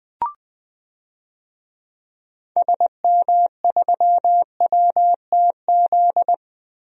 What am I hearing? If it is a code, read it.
Morse code, heard as a keyed tone, about 20 wpm, 700 Hz. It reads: SM3WTZ
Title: DXCC-100 Call Signs - 6 Characters - Repeated 2x - 20wpm